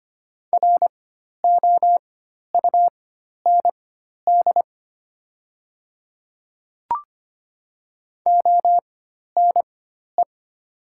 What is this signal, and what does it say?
Morse code, heard as a keyed tone, about 25 wpm, spaced as for 12 wpm, 700 Hz. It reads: ROUND ONE
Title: 4x Character Spacing - Top 200 Words - 25wpm